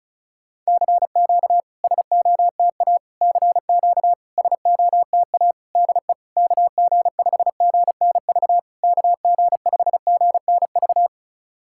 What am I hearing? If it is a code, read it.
Morse code, heard as a keyed tone, about 36 wpm, 700 Hz. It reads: CQ SOTA CQ SOTA DE KG5GNV KG5GNV
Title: SOTA QSO Practice 35wpm